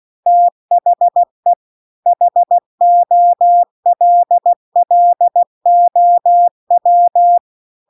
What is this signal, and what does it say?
Morse code, heard as a keyed tone, about 16 wpm, 700 Hz. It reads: THE HOLLOW